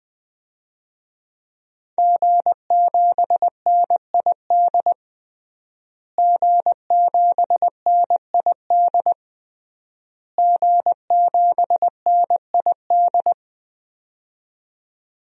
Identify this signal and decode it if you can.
Morse code, heard as a keyed tone, about 21 wpm, 700 Hz. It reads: G7NID G7NID G7NID
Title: DXCC-100 Call Signs - 5 Characters - Repeated 3x - 20wpm